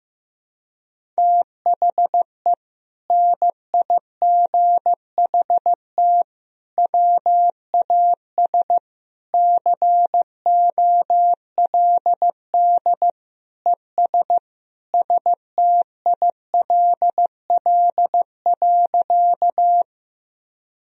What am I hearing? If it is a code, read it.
Morse code, heard as a keyed tone, about 15 wpm, 700 Hz. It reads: THE NIGHT WAS COLD ES STILL.
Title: Sentences from Top 400 Words and Abbreviations 15wpm